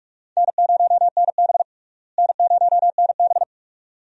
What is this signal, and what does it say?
Morse code, heard as a keyed tone, about 45 wpm, 700 Hz. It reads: N0NB N0NB
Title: USA Call Signs - 4 Characters - Repeated 2x - 45wpm